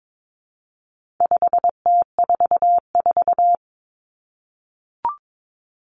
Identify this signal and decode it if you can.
Morse code, heard as a keyed tone, about 22 wpm, 700 Hz. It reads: 5T44
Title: Common Cut Numbers - 4-Digit 22wpm